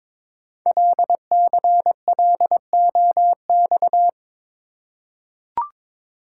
Morse code, heard as a keyed tone, about 22 wpm, 700 Hz. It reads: LCLOX
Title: Code Groups - 5 Letter:Numbers - 22wpm